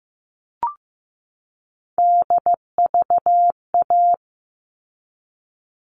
Morse code, heard as a keyed tone, about 15 wpm, 700 Hz. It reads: DVA
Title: All Cut Numbers - 3-Digit 15wpm